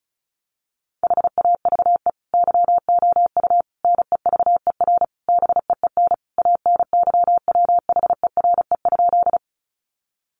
Morse code, heard as a keyed tone, about 34 wpm, 700 Hz. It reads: HAVE YOU NEVER BEEN ANYWHERE?
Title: Sentences from Top 1400 Words 35wpm